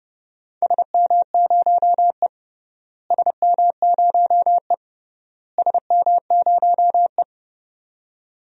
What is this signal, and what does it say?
Morse code, heard as a keyed tone, about 29 wpm, 700 Hz. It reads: SM0E SM0E SM0E
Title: DXCC-100 Call Signs - 4 Characters - Repeated 3x - 30wpm